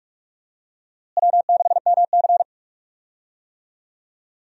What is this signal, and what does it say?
Morse code, heard as a keyed tone, about 47 wpm, 700 Hz. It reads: W6MC
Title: USA Call Signs - 4 Characters - Repeated 2x - 45wpm